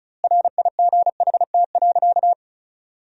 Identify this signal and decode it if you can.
Morse code, heard as a keyed tone, about 36 wpm, 700 Hz. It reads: RIGHT.
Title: 2x Word-Spacing - Sentences from Top 500 Words - 35wpm